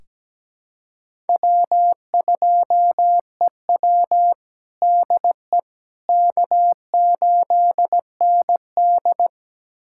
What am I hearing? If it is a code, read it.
Morse code, heard as a keyed tone, about 17 wpm, 700 Hz. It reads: W2EW DE K8ND
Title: USA Callsign DE Callsign - 4 Characters - 17wpm